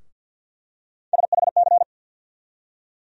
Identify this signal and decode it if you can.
Morse code, heard as a keyed tone, about 47 wpm, 700 Hz. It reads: SHC
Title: USA Call Sign Postfixes - 3 Letters 50wpm